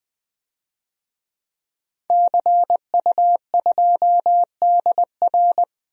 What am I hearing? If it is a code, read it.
Morse code, heard as a keyed tone, about 20 wpm, 700 Hz. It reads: CU2DR
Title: DXCC-100 Call Signs - 5 Characters - 20wpm